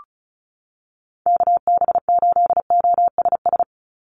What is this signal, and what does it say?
Morse code, heard as a keyed tone, about 34 wpm, 700 Hz. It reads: KB8OSS
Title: USA Call Signs 35wpm